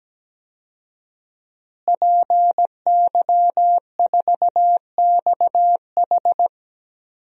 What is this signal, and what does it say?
Morse code, heard as a keyed tone, about 17 wpm, 700 Hz. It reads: PY4XH
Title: DXCC-100 Call Signs - 5 Characters - Repeated 3x - 17wpm